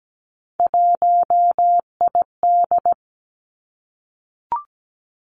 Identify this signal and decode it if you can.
Morse code, heard as a keyed tone, about 17 wpm, 700 Hz. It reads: 1ID
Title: Three Challenging Letter-Number Combinations 17wpm